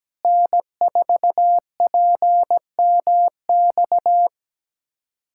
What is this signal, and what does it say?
Morse code, heard as a keyed tone, about 17 wpm, 700 Hz. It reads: N4PMX
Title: USA Callsign DE Callsign - 5 Characters - 17wpm